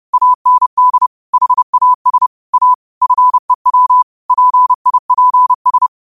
Morse code, heard as a keyed tone, about 30 wpm, 1000 Hz. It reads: AND HAS A FEW PIPS
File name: qotdfiles / 30wpm